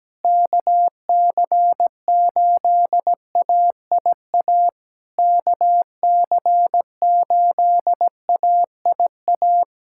Morse code, heard as a keyed tone, about 17 wpm, 700 Hz. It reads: KC8AIA KC8AIA
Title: SOTA QSO Practice 17wpm